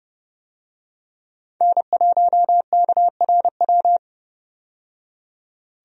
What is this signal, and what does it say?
Morse code, heard as a keyed tone, about 32 wpm, 700 Hz. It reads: N1KRW
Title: USA Call Signs - 5 Characters - 30wpm